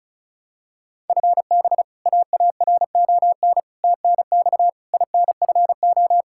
Morse code, heard as a keyed tone, about 36 wpm, 700 Hz. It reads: FB AARON TNX INFO
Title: K1USN SST QSO Practice 35wpm